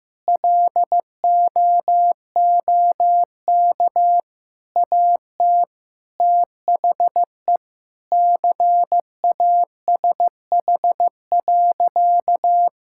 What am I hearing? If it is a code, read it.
Morse code, heard as a keyed tone, about 15 wpm, 700 Hz. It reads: LOOK AT THE CASH.
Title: Sentences from Top 1000 Words - Review Part 5 - 15wpm